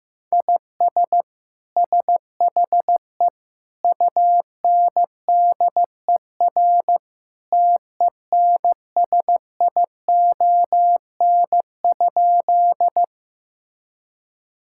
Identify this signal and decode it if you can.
Morse code, heard as a keyed tone, about 15 wpm, 700 Hz. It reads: IS SHE UNDER TENSION?
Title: Sentences from Top 1600 Words 15wpm